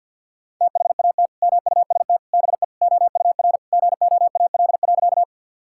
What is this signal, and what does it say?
Morse code, heard as a keyed tone, about 49 wpm, 700 Hz. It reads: THAT MUST BE OUR GOAL.
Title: Sentences from Top 1000 Words - Review Part 5 - 50wpm